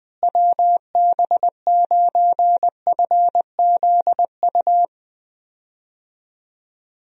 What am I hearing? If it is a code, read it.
Morse code, heard as a keyed tone, about 20 wpm, 700 Hz. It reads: WB9FZU
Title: USA Call Signs - 6 Characters - Repeated 2x - 20wpm